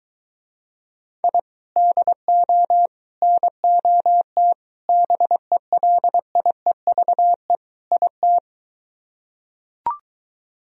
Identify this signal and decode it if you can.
Morse code, heard as a keyed tone, about 23 wpm, 700 Hz. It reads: I DO NOT BELIEVE IT